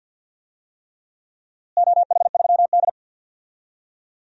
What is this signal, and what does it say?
Morse code, heard as a keyed tone, about 47 wpm, 700 Hz. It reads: OH3D
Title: DXCC-100 Call Signs - 4 Characters - Repeated 3x - 50wpm